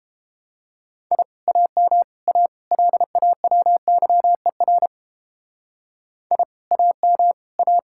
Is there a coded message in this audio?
Yes: Morse code from 1.1 to 7.8 s, heard as a keyed tone, about 31 wpm, 700 Hz. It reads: I AM A LAWYER I AM A